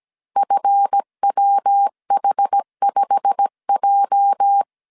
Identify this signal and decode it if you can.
Morse code, heard as a keyed tone, about 17 wpm, 800 Hz. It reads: FWH5J
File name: CW / DET / 1 9 1500C S C 16wpm 2909